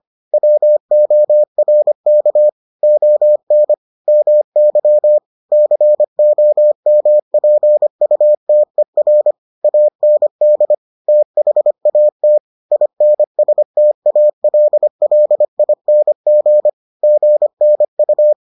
Morse code, heard as a keyed tone, about 25 wpm, 600 Hz. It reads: WORK ON MY COMPUTER AND THAT INSTALLING GNU